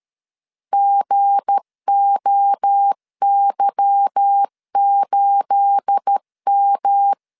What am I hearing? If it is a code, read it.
Morse code, heard as a keyed tone, about 13 wpm, 800 Hz. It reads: GOY8M
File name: LEZIONI / 10 SIGNS / 10C DOT 12wpm 2451